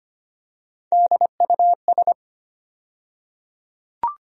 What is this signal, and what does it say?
Morse code, heard as a keyed tone, about 25 wpm, 700 Hz. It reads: DUS
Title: USA Call Sign Postfixes - 3 Letters 25wpm